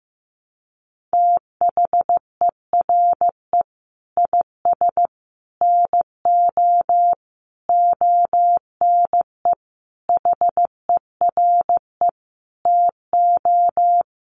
Morse code, heard as a keyed tone, about 15 wpm, 700 Hz. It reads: THERE IS NO ONE HERE TO
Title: Sentences from Top 200 Words 15wpm